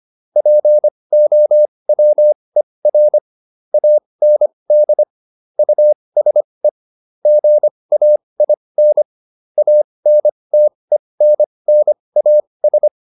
Morse code, heard as a keyed tone, about 25 wpm, spaced as for 20 wpm, 600 Hz. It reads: POWER AND USE GAIN ANTENNAS